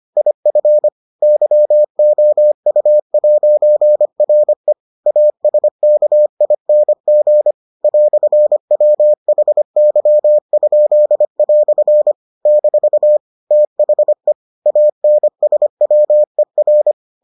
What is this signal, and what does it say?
Morse code, heard as a keyed tone, about 25 wpm, 600 Hz. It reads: IF YOU'RE ASKING "WHY?" - THE ANSWER